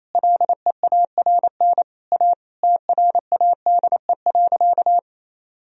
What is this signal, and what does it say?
Morse code, heard as a keyed tone, about 28 wpm, 700 Hz. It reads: LEARN A TRADE.
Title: Sentences from Top 1000 Words - Review Part 4 - 28wpm